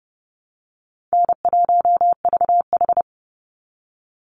Morse code, heard as a keyed tone, about 28 wpm, 700 Hz. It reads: N1VH
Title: Call Signs Repeated 3x - 30wpm